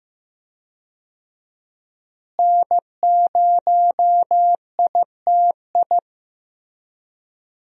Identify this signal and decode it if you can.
Morse code, heard as a keyed tone, about 15 wpm, 700 Hz. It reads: N0ITI
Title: USA Call Signs - 5 Characters - Repeated 2x - 15wpm